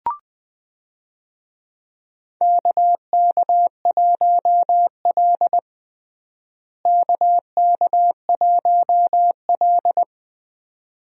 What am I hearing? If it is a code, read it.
Morse code, heard as a keyed tone, about 20 wpm, 700 Hz. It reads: KK1L KK1L
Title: USA Call Signs - 4 Characters - Repeated 2x - 20wpm